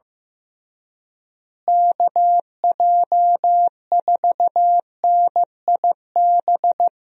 Morse code, heard as a keyed tone, about 15 wpm, 700 Hz. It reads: KJ4NIB
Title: USA Call Signs - 6 Characters - Repeated 3x - 15wpm